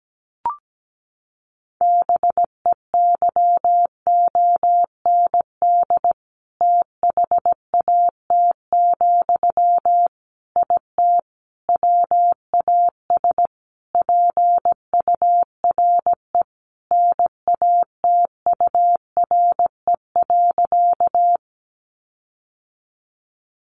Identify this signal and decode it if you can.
Morse code, heard as a keyed tone, about 17 wpm, 700 Hz. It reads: BEYOND THAT, IT WAS PURE NATURE.
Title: Sentences from Top 1500 Words 17wpm